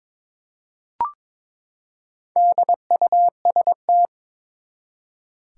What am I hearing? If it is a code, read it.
Morse code, heard as a keyed tone, about 22 wpm, 700 Hz. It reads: DUST
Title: Four Letter Words - 22wpm